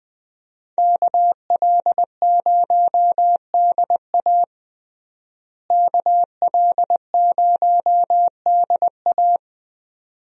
Morse code, heard as a keyed tone, about 20 wpm, 700 Hz. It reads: KL0DA KL0DA
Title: USA Call Signs - 5 Characters - Repeated 2x - 20wpm